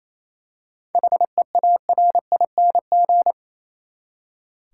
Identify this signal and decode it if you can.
Morse code, heard as a keyed tone, about 28 wpm, 700 Hz. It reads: HEARING
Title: Seven Letter Words - 28wpm